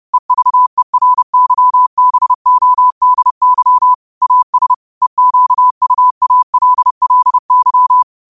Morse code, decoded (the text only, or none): EVERYBODY AS EQUALLY